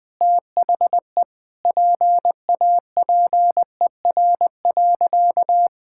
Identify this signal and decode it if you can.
Morse code, heard as a keyed tone, about 20 wpm, 700 Hz. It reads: THE PAPER.
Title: Sentences with 5-Word Phrases - Spoken Beginning - 20wpm